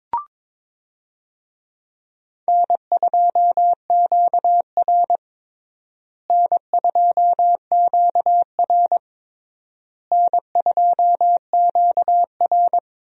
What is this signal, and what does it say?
Morse code, heard as a keyed tone, about 22 wpm, 700 Hz. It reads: N2QR N2QR N2QR
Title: USA Call Signs - 4 Characters - Repeated 3x - 22wpm